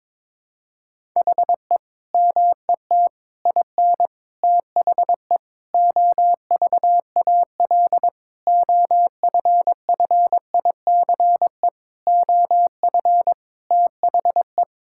Morse code, heard as a keyed tone, about 22 wpm, 700 Hz. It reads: HE MET IN THE OVAL OFFICE OF THE
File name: practice / 22wpm / Sentences with 8-Word Phrases - 22wpm